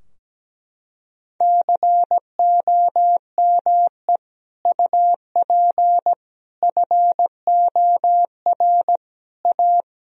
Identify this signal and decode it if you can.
Morse code, heard as a keyed tone, about 17 wpm, 700 Hz. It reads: COME UP FOR A
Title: Sentences from Top 1000 Words - Review Part 2 - 17wpm